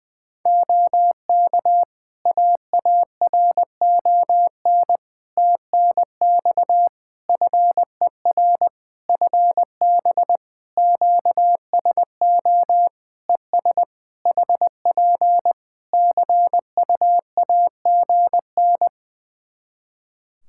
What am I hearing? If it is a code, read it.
Morse code, heard as a keyed tone, about 20 wpm, 700 Hz. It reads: OK AARON TNX FER FB QSO ES HP CUAGN
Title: K1USN SST QSO Practice 20wpm